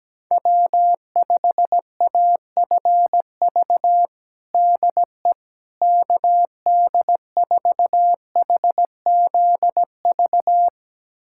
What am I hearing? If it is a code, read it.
Morse code, heard as a keyed tone, about 17 wpm, 700 Hz. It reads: W5AFV DE KD4HZV
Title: USA Callsign DE Callsign 17wpm